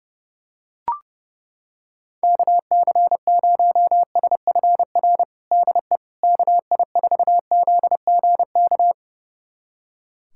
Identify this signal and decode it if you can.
Morse code, heard as a keyed tone, about 32 wpm, 700 Hz. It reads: KC0SFR DE KI4ZGK
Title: USA Callsign DE Callsign - 6 Characters - 30wpm